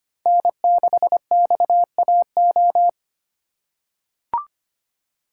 Morse code, heard as a keyed tone, about 25 wpm, 700 Hz. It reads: N6XAO
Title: USA Call Signs - 3x Character Spacing - 25wpm